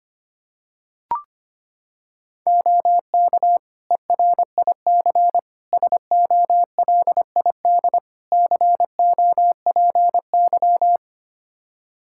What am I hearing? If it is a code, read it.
Morse code, heard as a keyed tone, about 25 wpm, 700 Hz. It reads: OK ERIC SOLID COPY
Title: Practice QSO - Exchange 3 - More Info - 25wpm